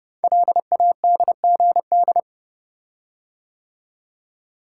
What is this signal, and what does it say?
Morse code, heard as a keyed tone, about 30 wpm, 700 Hz. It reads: LADGD
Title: Code Groups - 5 Letters - 30wpm